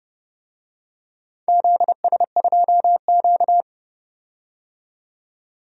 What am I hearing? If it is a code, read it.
Morse code, heard as a keyed tone, about 29 wpm, 700 Hz. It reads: ZS2Q
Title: DXCC-100 Call Signs - 4 Characters - 30wpm